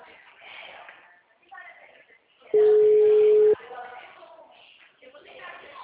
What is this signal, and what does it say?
Ringback tone (Europe, 425 Hz), with speech.